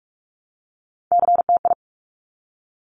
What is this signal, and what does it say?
Morse code, heard as a keyed tone, about 46 wpm, 700 Hz. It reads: CTI
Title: Top Three Letter Combinations in Words 45wpm